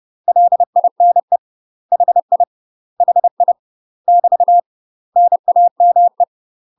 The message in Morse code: LINE HI HI <BT> NAME

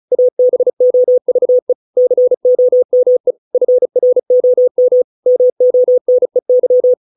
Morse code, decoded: ABOVE COME FROM MONEY

